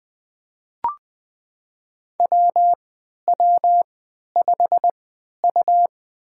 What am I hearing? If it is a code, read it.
Morse code, heard as a keyed tone, about 20 wpm, spaced as for 11 wpm, 700 Hz. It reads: WW5U
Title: USA Call Signs - 4 Characters - 3x Character Spacing - 20wpm